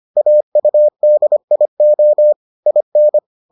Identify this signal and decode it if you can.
Morse code, heard as a keyed tone, about 25 wpm, 600 Hz. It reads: AUDIO IN